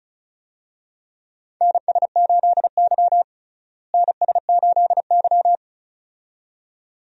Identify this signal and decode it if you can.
Morse code, heard as a keyed tone, about 34 wpm, 700 Hz. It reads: NS8Y NS8Y
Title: USA Call Signs - 4 Characters - Repeated 2x - 35wpm